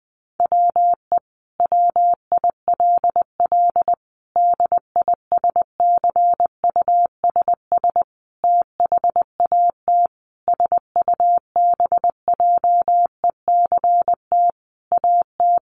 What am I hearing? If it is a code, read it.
Morse code, heard as a keyed tone, about 20 wpm, 700 Hz. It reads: WE WILL DISCUSS THAT SUBJECT AT
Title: Sentences from Top 1000 Words 20wpm